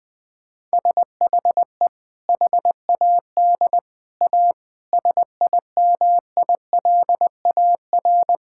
Morse code, heard as a keyed tone, about 21 wpm, 700 Hz. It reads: SHE HAD A SIMILAR
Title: Sentences from Top 1000 Words - Review Part 5 - 20wpm